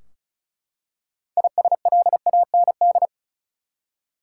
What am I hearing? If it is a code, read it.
Morse code, heard as a keyed tone, about 36 wpm, 700 Hz. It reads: ISLAND
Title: Six Letter Words - 35wpm